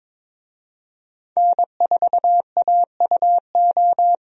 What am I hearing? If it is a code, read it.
Morse code, heard as a keyed tone, about 22 wpm, 700 Hz. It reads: N4AUO
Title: USA Call Signs - 5 Characters - 22wpm